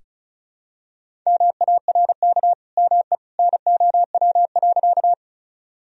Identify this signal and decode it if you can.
Morse code, heard as a keyed tone, about 36 wpm, 700 Hz. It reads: MARK ME NOW.
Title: Sentences from Top 1000 Words - Review Part 1 - 35wpm